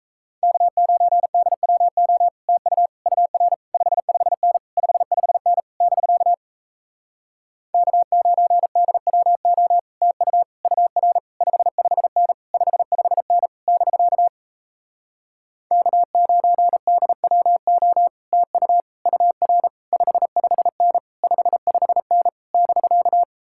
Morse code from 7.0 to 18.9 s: K9DWO TU UR 55N 55N <BK> K9DWO TU

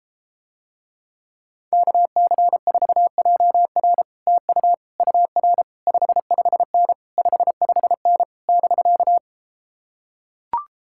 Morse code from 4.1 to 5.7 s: TU UR

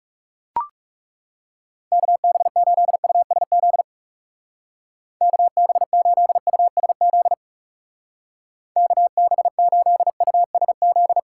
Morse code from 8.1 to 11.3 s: KB8USZ

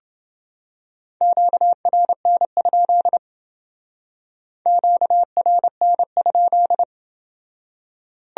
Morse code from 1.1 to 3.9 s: QRN?

